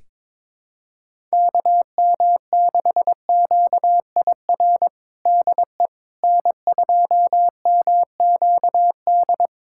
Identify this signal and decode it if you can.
Morse code, heard as a keyed tone, about 22 wpm, 700 Hz. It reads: KM6QIR DE N2MQD
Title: USA Callsign DE Callsign 22wpm